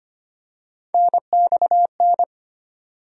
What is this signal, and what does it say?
Morse code, heard as a keyed tone, about 25 wpm, 700 Hz. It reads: NXN
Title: USA Call Sign Postfixes - 3 Letters 25wpm